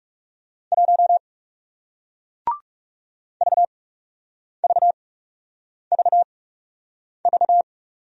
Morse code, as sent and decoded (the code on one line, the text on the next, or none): .---- / ...- ...- ...- ...-
1 VVVV